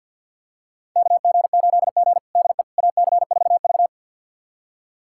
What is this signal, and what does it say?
Morse code, heard as a keyed tone, about 47 wpm, 700 Hz. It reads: KG8G DE AC4V
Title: USA Callsign DE Callsign - 4 Characters - 50wpm